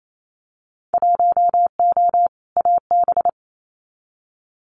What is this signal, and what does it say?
Morse code, heard as a keyed tone, about 28 wpm, 700 Hz. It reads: 1O AB
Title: Winter Field Day Exchanges Encore 28wpm